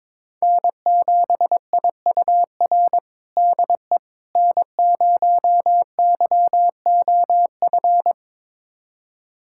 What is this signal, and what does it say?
Morse code, heard as a keyed tone, about 22 wpm, 700 Hz. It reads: N7IUR DE N0YOF
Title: USA Callsign DE Callsign - 5 Characters - 22wpm